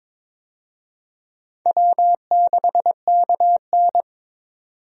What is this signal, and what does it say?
Morse code, heard as a keyed tone, about 22 wpm, 700 Hz. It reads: W6KN
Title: USA Call Signs - 4 Characters - 22wpm